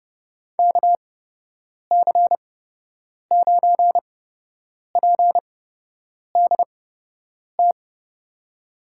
Morse code, heard as a keyed tone, about 30 wpm, spaced as for 8 wpm, 700 Hz. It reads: KC9PDT
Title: USA Call Signs - 6 Characters - 8x Character Spacing - 30wpm